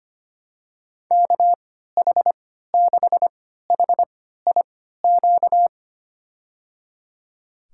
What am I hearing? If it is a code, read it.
Morse code, heard as a keyed tone, about 25 wpm, spaced as for 14 wpm, 700 Hz. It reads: KH6HIQ
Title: USA Call Signs - 6 Characters - 3x Character Spacing - 25wpm